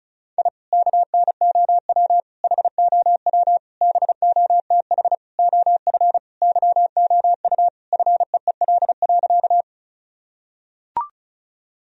Morse code, heard as a keyed tone, about 34 wpm, 700 Hz. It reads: I KNOW HOW BOTH OF YOU FEEL.
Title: Sentences from Top 1000 Words - Review Part 1 - 35wpm